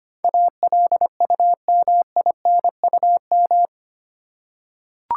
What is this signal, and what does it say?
Morse code, heard as a keyed tone, about 25 wpm, 700 Hz. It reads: ALUMINUM
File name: 8 Letter Words - First 3 Letters Given - 25wpm